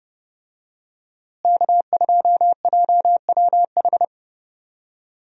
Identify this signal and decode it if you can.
Morse code, heard as a keyed tone, about 32 wpm, 700 Hz. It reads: K2JWH
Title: USA Call Signs - 5 Characters - Repeated 2x - 30wpm